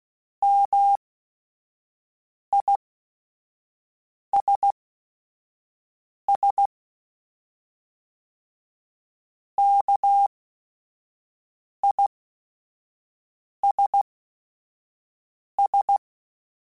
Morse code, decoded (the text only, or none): MISS KISS